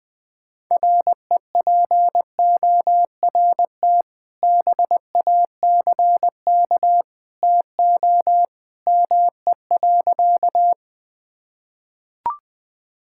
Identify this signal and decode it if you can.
Morse code, heard as a keyed tone, about 19 wpm, 700 Hz. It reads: REPORT BACK TO ME.
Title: Sentences from Top 1000 Words - Review Part 6 - 20wpm